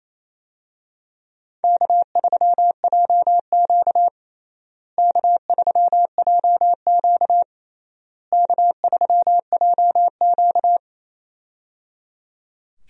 Morse code, heard as a keyed tone, about 27 wpm, 700 Hz. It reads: K3JQ K3JQ K3JQ
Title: USA Call Signs - 4 Characters - Repeated 3x - 28wpm